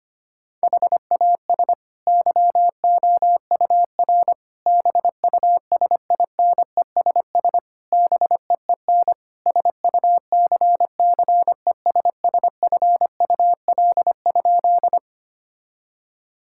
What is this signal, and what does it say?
Morse code, heard as a keyed tone, about 25 wpm, 700 Hz. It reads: HAS YOUR BUSINESS BEEN SUCCESSFUL?
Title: Sentences from Top 1000 Words - Review Part 6 - 25wpm